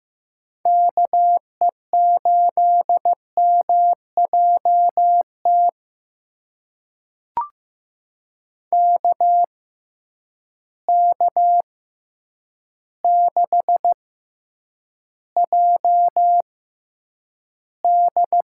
Morse code, heard as a keyed tone, about 15 wpm, spaced as for 8 wpm, 700 Hz. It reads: KE8MJT KK6JD
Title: USA Call Signs - 6 Characters - 6x Character Spacing - 15wpm